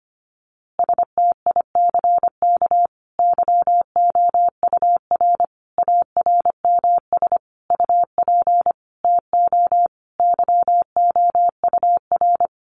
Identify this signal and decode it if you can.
Morse code, heard as a keyed tone, about 25 wpm, 700 Hz. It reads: STICK YOUR ARMS UP TO YOUR
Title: A Sound Of Thunder 25wpm